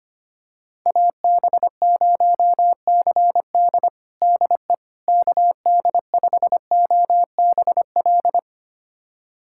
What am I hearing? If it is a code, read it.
Morse code, heard as a keyed tone, about 25 wpm, 700 Hz. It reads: AB0CD DE KD5OBL
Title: USA Callsign DE Callsign 25wpm